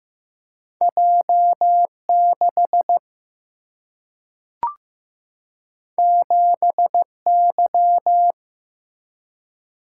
Morse code, heard as a keyed tone, about 15 wpm, 700 Hz. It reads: J6 7Y
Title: Single-Letter Single-Number 15wpm